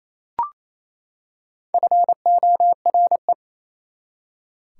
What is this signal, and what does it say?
Morse code, heard as a keyed tone, about 28 wpm, 700 Hz. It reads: FORE
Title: Prefixes and Suffixes 28wpm